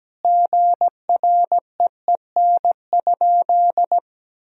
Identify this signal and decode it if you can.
Morse code, heard as a keyed tone, about 17 wpm, 700 Hz. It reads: GREEN?
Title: Sentences from Top 1000 Words - Review Part 2 - 17wpm